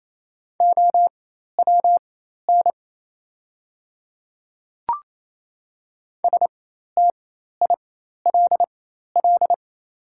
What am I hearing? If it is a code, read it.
Morse code, heard as a keyed tone, about 28 wpm, spaced as for 13 wpm, 690 Hz. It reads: OWN STILL
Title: 4x Character Spacing - Top 300 Words - 28wpm